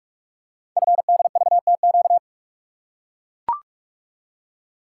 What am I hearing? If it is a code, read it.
Morse code, heard as a keyed tone, about 44 wpm, 700 Hz. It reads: FDVTQ
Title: Code Groups - 5 Letters - 45wpm